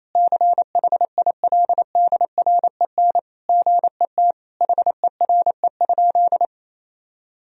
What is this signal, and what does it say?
Morse code, heard as a keyed tone, about 28 wpm, 700 Hz. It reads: CHILDREN GET HERE?
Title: Sentences from Top 1000 Words - Review Part 1 - 28wpm